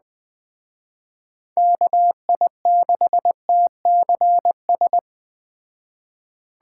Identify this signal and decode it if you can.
Morse code, heard as a keyed tone, about 20 wpm, 700 Hz. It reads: KI6TCS
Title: Speed Racing - USA 2x3 Call Signs 17wpm